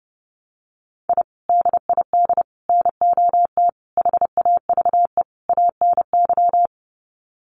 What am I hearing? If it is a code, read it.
Morse code, heard as a keyed tone, about 30 wpm, 700 Hz. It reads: I DID NOT HAVE ANY
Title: Sets of 5 Words Encore 30wpm